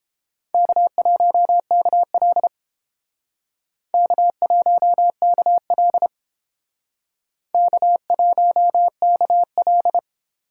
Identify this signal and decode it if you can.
Morse code, heard as a keyed tone, about 30 wpm, 700 Hz. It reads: K1KL K1KL K1KL